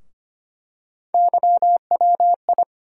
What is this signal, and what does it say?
Morse code, heard as a keyed tone, about 25 wpm, 700 Hz. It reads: YWI
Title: USA Call Sign Postfixes - 3 Letters 25wpm